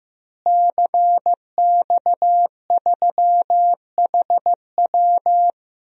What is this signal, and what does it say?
Morse code, heard as a keyed tone, about 15 wpm, 700 Hz. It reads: CX3HW
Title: DXCC-100 Call Signs - 5 Characters - Repeated 3x - 15wpm